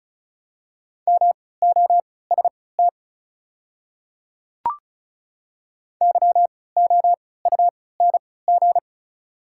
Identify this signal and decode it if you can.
Morse code, heard as a keyed tone, about 36 wpm, spaced as for 20 wpm, 690 Hz. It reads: MOST YOUNG
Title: 3x Character Spacing - Top 500 Words - 35wpm